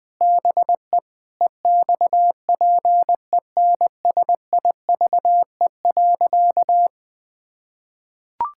Morse code, heard as a keyed tone, about 19 wpm, 700 Hz. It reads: BE EXPENSIVE.
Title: Sentences from Top 1000 Words - Review Part 6 - 20wpm